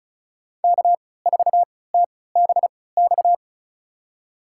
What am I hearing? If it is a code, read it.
Morse code, heard as a keyed tone, about 36 wpm, spaced as for 20 wpm, 700 Hz. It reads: K4TBX
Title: USA Call Signs - 5 Characters - 3x Character Spacing - 35wpm